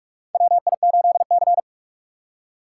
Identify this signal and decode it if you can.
Morse code, heard as a keyed tone, about 44 wpm, 690 Hz. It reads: WI8C
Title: USA Call Signs - 4 Characters - 3x Character Spacing - 45wpm